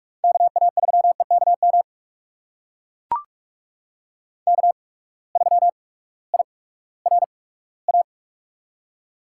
Morse code, heard as a keyed tone, about 42 wpm, spaced as for 20 wpm, 700 Hz. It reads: KA3EKM K3IRA
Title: USA Call Signs - 8x Character Spacing - 45wpm